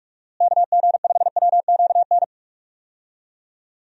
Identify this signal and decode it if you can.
Morse code, heard as a keyed tone, about 41 wpm, 700 Hz. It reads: KG5WQN
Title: USA Call Signs - 6 Characters - Repeated 3x - 45wpm